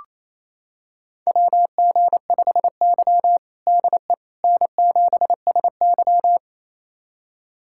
Morse code, heard as a keyed tone, about 28 wpm, 700 Hz. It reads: WG5Y DE N7SY
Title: USA Callsign DE Callsign - 4 Characters - 28wpm